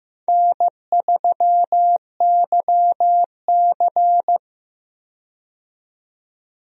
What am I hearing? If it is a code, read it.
Morse code, heard as a keyed tone, about 15 wpm, 700 Hz. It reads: N3YC
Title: USA Call Signs - 4 Characters - 15wpm